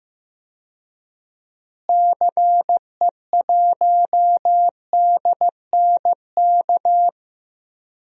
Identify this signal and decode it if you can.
Morse code, heard as a keyed tone, about 15 wpm, 700 Hz. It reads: CE1DNK
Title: DXCC-100 Call Signs - 6 Characters - Repeated 2x - 15wpm